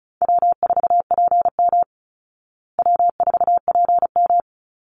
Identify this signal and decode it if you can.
Morse code, heard as a keyed tone, about 34 wpm, 700 Hz. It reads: W4PM W4PM
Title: Easy Call Signs Repeated 3x - 35wpm